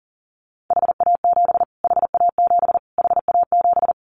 Morse code, heard as a keyed tone, about 40 wpm, 700 Hz. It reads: HA7 HA7 HA7
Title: Callsign Prefixes for Top 25 DXCC - Repeated 3x - 40wpm